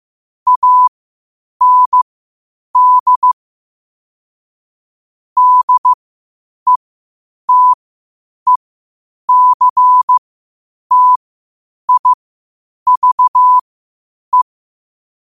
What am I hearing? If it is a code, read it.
Morse code, heard as a keyed tone, about 15 wpm, spaced as for 8 wpm, 1000 Hz. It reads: AND DETECTIVE
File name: qotdfiles / 8wpm